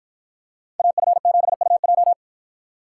Morse code, heard as a keyed tone, about 51 wpm, 700 Hz. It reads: AF7UJ